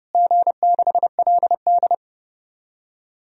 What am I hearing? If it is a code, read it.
Morse code, heard as a keyed tone, about 30 wpm, 700 Hz. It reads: G6LD
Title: DXCC-100 Call Signs - 4 Characters - Repeated 2x - 30wpm